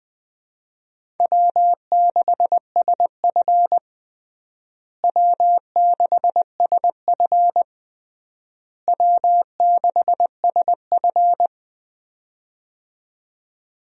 Morse code, heard as a keyed tone, about 20 wpm, 700 Hz. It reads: W6SF W6SF W6SF
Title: USA Call Signs - 4 Characters - Repeated 3x - 20wpm